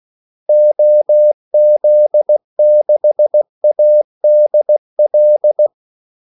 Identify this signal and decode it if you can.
Morse code, heard as a keyed tone, about 16 wpm, 600 Hz. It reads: OZ6ADL
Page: Kaldesignaler 16 wpm